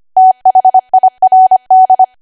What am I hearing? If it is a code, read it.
Morse code, heard as a keyed tone, about 25 wpm, 750 Hz. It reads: THIRD
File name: cwa / W1AW-Level3 / 25WPM